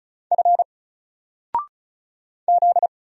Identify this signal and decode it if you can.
Morse code, heard as a keyed tone, about 36 wpm, 700 Hz. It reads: F Z
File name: Single-Letter-Number-35wpm